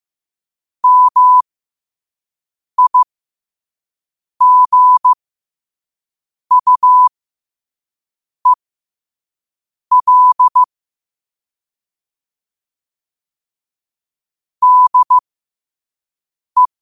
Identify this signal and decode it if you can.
Morse code, heard as a keyed tone, about 15 wpm, spaced as for 5 wpm, 1000 Hz. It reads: MIGUEL DE
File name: qotdfiles / 5wpm